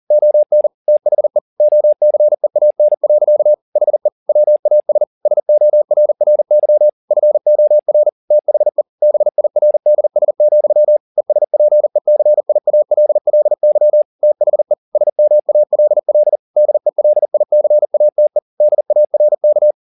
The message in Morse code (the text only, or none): ON THE OCEAN. HE WAS SORRY FOR THE BIRDS, ESPECIALLY THE SMALL DELICATE DARK